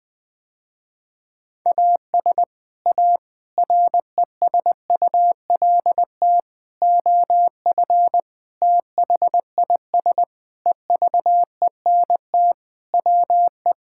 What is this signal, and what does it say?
Morse code, heard as a keyed tone, about 20 wpm, 700 Hz. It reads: AS A RESULT OF THIS EVENT WE
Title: Sentences with 5-Word Phrases - Spoken Beginning - 20wpm